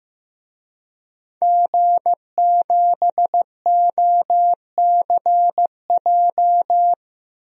Morse code, heard as a keyed tone, about 16 wpm, 700 Hz. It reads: G7OCJ
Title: DXCC-100 Call Signs - 5 Characters - Repeated 3x - 15wpm